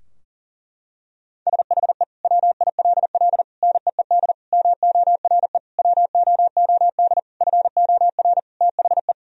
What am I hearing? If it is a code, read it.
Morse code, heard as a keyed tone, about 40 wpm, 700 Hz. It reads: SHE WILL NEED MORE WOOD FOR THE
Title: Sentences from Top 1000 Words - Review Part 2 - 40wpm